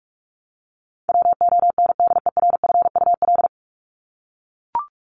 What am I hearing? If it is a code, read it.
Morse code, heard as a keyed tone, about 44 wpm, 700 Hz. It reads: WONDERFUL
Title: Top 1001-1250 Words 45wpm